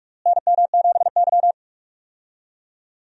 Morse code, heard as a keyed tone, about 46 wpm, 700 Hz. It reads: NM7Y
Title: USA Call Signs - 4 Characters - Repeated 2x - 45wpm